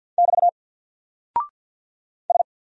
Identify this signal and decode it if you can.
Morse code, heard as a keyed tone, about 48 wpm, 700 Hz. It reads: <BT> S
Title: Single Character (Letter:Number:Punctuation:Prosign) 50wpm